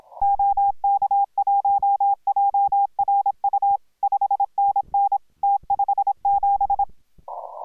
Morse code, heard as a keyed tone, about 26 wpm, 810 Hz. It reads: OK1JRU 5NN T57